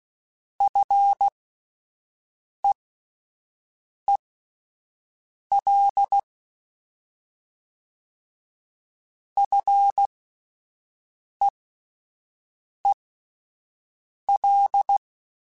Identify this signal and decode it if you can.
Morse code, heard as a keyed tone, about 16 wpm, spaced as for 5 wpm, 800 Hz. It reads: FEEL FEEL
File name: Fs / Single F mp3s / Single F Prefixes 3x2 Twice @5